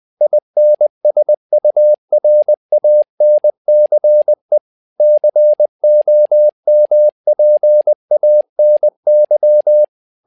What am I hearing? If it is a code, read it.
Morse code, heard as a keyed tone, about 20 wpm, 600 Hz. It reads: INSURANCE COMPANY
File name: learning center / tower trouble / tower trouble 20wpm